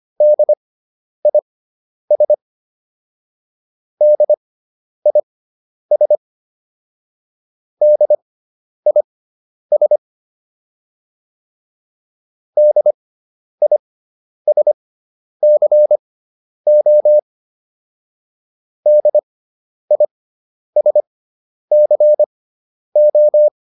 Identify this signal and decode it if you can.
Morse code, heard as a keyed tone, about 25 wpm, spaced as for 10 wpm, 600 Hz. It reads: DIS DIS DIS DISCO DISCO